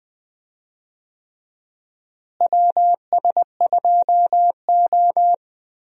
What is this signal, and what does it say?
Morse code, heard as a keyed tone, about 20 wpm, 700 Hz. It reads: WS2O
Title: USA Call Signs - 4 Characters - Repeated 3x - 20wpm